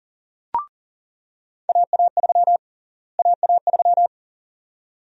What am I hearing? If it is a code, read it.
Morse code, heard as a keyed tone, about 40 wpm, 700 Hz. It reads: AA3 AA3
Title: USA Call Sign Prefixes - Repeated 2x - 40wpm